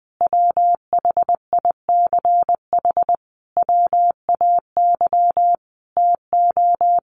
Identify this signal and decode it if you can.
Morse code, heard as a keyed tone, about 20 wpm, 700 Hz. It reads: WHICH WAY TO
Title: Sentences from Top 1300 Words 20wpm